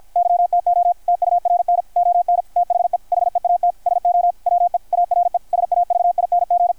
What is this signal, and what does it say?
Morse code, heard as a keyed tone, about 53 wpm, 700 Hz. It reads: YTO TURN ON THE HEAT SO WE ARE SAVING
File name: archived / CRSnet